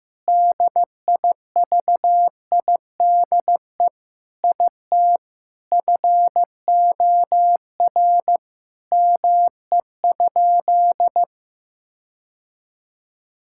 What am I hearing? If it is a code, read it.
Morse code, heard as a keyed tone, about 14 wpm, 700 Hz. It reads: DIVIDE IT FOR ME?
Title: Sentences from Top 1600 Words 15wpm